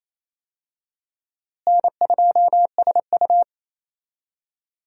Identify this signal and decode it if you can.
Morse code, heard as a keyed tone, about 28 wpm, 700 Hz. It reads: N2SU
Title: USA Call Signs - 4 Characters - Repeated 3x - 28wpm